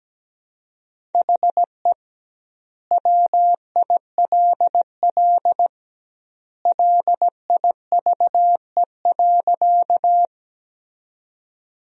Morse code, heard as a keyed tone, about 17 wpm, 700 Hz. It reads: HE WILL LIVE.
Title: 2x Word-Spacing - Sentences from Top 200 Words - 17wpm